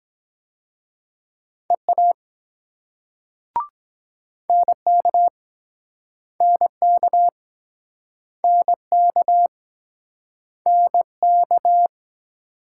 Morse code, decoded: EA NK NK NK NK

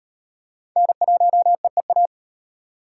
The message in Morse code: N1EEA

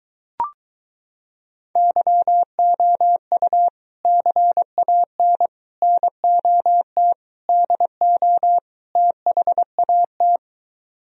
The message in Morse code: YOU CAN NOT DO THAT

